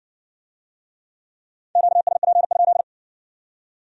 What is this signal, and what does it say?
Morse code, heard as a keyed tone, about 55 wpm, 700 Hz. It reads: QSP?